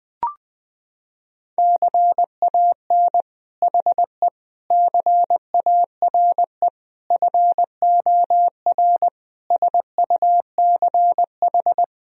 Morse code, heard as a keyed tone, about 19 wpm, 700 Hz. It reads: CAN HE CARE FOR SUCH
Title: Sentences from Top 1000 Words - Review Part 1 - 20wpm